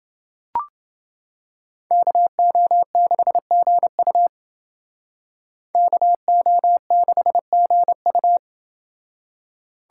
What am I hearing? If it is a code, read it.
Morse code, heard as a keyed tone, about 28 wpm, 700 Hz. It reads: KO6GU KO6GU